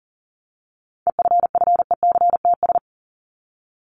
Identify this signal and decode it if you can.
Morse code, heard as a keyed tone, about 40 wpm, 700 Hz. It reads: EFFECTS
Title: ES:S Suffix - Top 100 Words with Base Words 40wpm